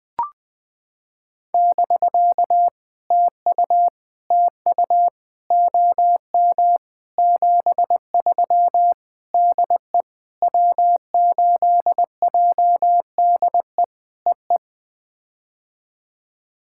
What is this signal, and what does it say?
Morse code, heard as a keyed tone, about 19 wpm, 700 Hz. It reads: <BK> TU TU OM 73 DE W8JDE EE
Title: POTA QSO Practice - 20wpm